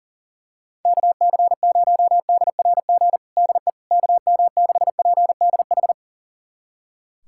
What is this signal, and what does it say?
Morse code, heard as a keyed tone, about 40 wpm, 700 Hz. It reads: KC0DRG DE KM6PDH
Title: USA Callsign DE Callsign - 6 Characters - 40wpm